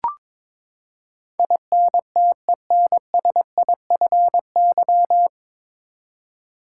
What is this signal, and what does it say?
Morse code, heard as a keyed tone, about 22 wpm, 700 Hz. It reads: INTENSIFY
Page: Morse Code Ninja practice 22wpm